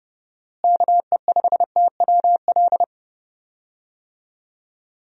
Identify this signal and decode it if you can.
Morse code, heard as a keyed tone, about 30 wpm, 700 Hz. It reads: KE5TWL
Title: USA Call Signs - 6 Characters - Repeated 3x - 30wpm